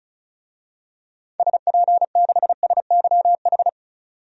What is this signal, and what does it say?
Morse code, heard as a keyed tone, about 36 wpm, 700 Hz. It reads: SP6SYH
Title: DXCC-100 Call Signs - 6 Characters - 35wpm